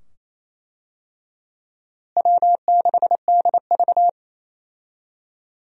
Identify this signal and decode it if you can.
Morse code, heard as a keyed tone, about 28 wpm, 700 Hz. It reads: W6DV
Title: USA Call Signs - 4 Characters - Repeated 2x - 28wpm